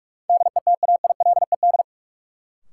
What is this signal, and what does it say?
Morse code, heard as a keyed tone, about 46 wpm, 700 Hz. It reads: DETAILED
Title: Sentences from Top 1500 Words - New Word Drill - 45wpm